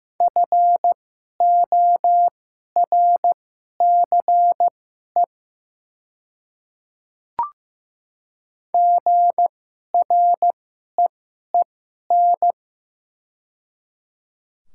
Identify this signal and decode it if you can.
Morse code, heard as a keyed tone, about 15 wpm, spaced as for 11 wpm, 700 Hz. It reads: FORCE GREEN
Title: 2x Character Spacing - Top 500 Words - 15wpm